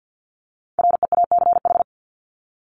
Morse code, heard as a keyed tone, about 49 wpm, 700 Hz. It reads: REACH
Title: Top 500 Words 50wpm